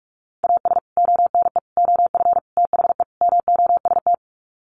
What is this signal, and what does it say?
Morse code, heard as a keyed tone, about 44 wpm, 700 Hz. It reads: AS ONE OF THE MOST
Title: Sets of 5 Words 45wpm